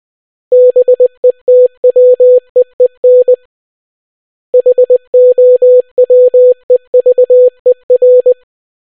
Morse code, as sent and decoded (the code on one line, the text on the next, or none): -... - .-- . -. / .... --- .-- . ...- . .-.
BETWEEN HOWEVER